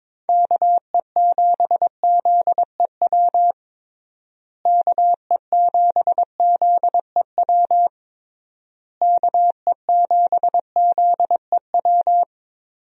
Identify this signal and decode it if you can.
Morse code, heard as a keyed tone, about 22 wpm, 700 Hz. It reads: KE7ZEW KE7ZEW KE7ZEW
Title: USA Call Signs - 6 Characters - Repeated 3x - 22wpm